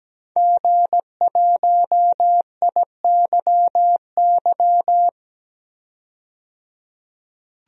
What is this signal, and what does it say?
Morse code, heard as a keyed tone, about 17 wpm, 700 Hz. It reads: G1IYY
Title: DXCC-100 Call Signs - 5 Characters - 17wpm